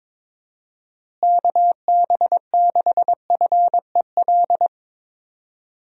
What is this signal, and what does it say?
Morse code, heard as a keyed tone, about 22 wpm, 700 Hz. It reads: KB6FEL
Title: USA Call Signs - 6 Characters - Repeated 2x - 22wpm